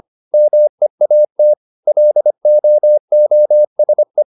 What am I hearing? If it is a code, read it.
Morse code, heard as a keyed tone, about 25 wpm, 600 Hz. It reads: MEAT LOOSE